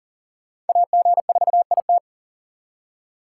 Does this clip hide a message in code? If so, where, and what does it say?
Yes: Morse code from 0.7 to 2.0 s, heard as a keyed tone, about 51 wpm, 700 Hz. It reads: AG4IT